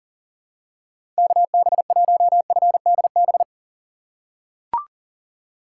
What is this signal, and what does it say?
Morse code, heard as a keyed tone, about 37 wpm, 700 Hz. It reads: KB1FDB
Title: USA Call Signs - 6 Characters - 3x Character Spacing - 40wpm